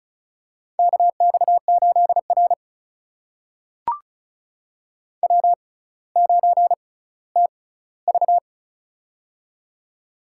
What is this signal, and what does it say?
Morse code, heard as a keyed tone, about 36 wpm, spaced as for 18 wpm, 700 Hz. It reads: KX8R W9TV
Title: USA Call Signs - 4 Characters - 6x Character Spacing - 35wpm